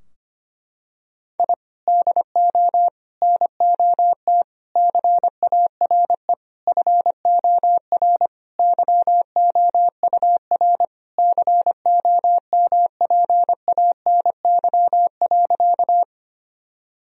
Morse code, heard as a keyed tone, about 25 wpm, 700 Hz. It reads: I DO NOT CARE FOR YOUR COMPANY.
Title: Sentences from Top 1000 Words - Review Part 3 - 25wpm